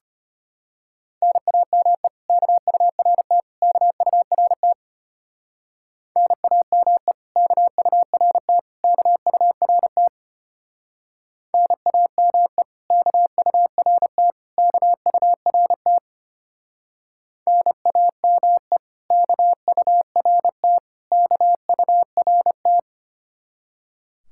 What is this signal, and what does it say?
Morse code, heard as a keyed tone, about 31 wpm, 700 Hz. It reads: NAME KURT KURT NAME KURT KURT NAME KURT KURT NAME KURT KURT
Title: Speed-Racing Common QSO Phrases Encore - 25wpm